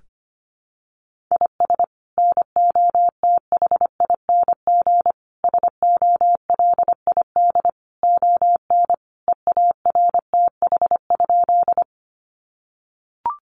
Morse code, heard as a keyed tone, about 25 wpm, 700 Hz. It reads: IS NOTHING SOLID ON EARTH?
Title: Sentences from Top 1500 Words 25wpm